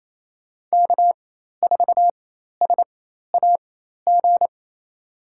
Morse code, heard as a keyed tone, about 28 wpm, spaced as for 13 wpm, 690 Hz. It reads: K4SAG
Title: USA Call Signs - 5 Characters - 4x Character Spacing - 28wpm